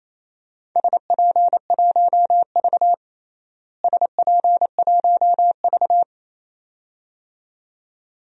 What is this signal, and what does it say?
Morse code, heard as a keyed tone, about 28 wpm, 700 Hz. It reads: SP1V SP1V
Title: DXCC-100 Call Signs - 4 Characters - Repeated 2x - 28wpm